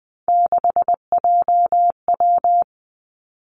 Morse code, heard as a keyed tone, about 20 wpm, 700 Hz. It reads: 6JW
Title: Three Challenging Letter-Number Combinations 20wpm